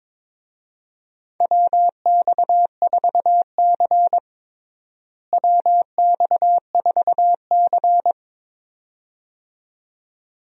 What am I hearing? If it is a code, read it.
Morse code, heard as a keyed tone, about 22 wpm, 700 Hz. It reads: WX4C WX4C
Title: USA Call Signs - 4 Characters - Repeated 2x - 22wpm